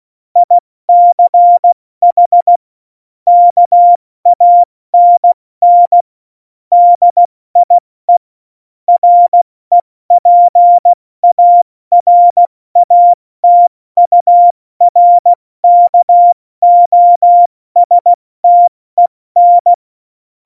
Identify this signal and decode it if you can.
Morse code, heard as a keyed tone, about 16 wpm, 700 Hz. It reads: ICH KANN DIE REPARATURKOSTEN
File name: Morsekurs / Witze / Witz 02 Juni 2020, 70 F80